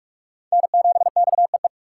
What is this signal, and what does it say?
Morse code, heard as a keyed tone, about 41 wpm, 700 Hz. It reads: N7XEE